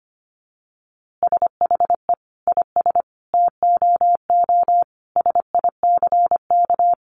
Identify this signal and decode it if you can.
Morse code, heard as a keyed tone, about 25 wpm, 700 Hz. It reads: SHE IS TOO SICK
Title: Easy Phrases 25wpm